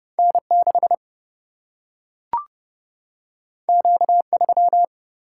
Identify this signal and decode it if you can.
Morse code, heard as a keyed tone, about 31 wpm, 700 Hz. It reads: N 6 Q 3
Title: Single-Letter Single-Number 30wpm